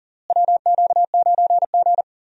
Morse code, heard as a keyed tone, about 40 wpm, 700 Hz. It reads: WQ9G
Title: USA Call Signs - 4 Characters - Repeated 3x - 40wpm